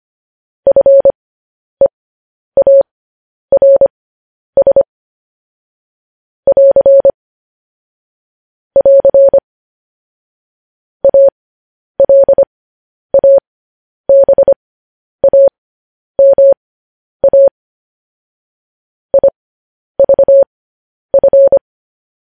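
Morse code, heard as a keyed tone, about 25 wpm, spaced as for 10 wpm, 550 Hz. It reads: FEARS <AR> <AR> ALABAMA IVF
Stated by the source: Morse code news headlines practice, at 10 WPM.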